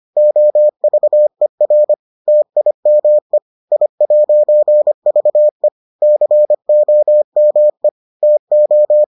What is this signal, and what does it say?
Morse code, heard as a keyed tone, about 25 wpm, 600 Hz. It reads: OVER TIME I'VE COME TO